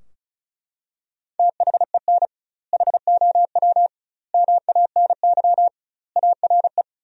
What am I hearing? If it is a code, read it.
Morse code, heard as a keyed tone, about 35 wpm, 700 Hz. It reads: THEN HOW MANY ARE
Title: 2x Word-Spacing - Sentences from Top 100 Words - 35wpm